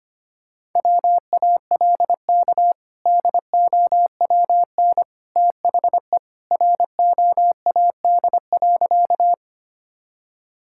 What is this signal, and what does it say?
Morse code, heard as a keyed tone, about 25 wpm, 700 Hz. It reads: WALK DOWN THE ROAD.
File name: practice / 25wpm / Sentences with 2-Word Phrases - 25wpm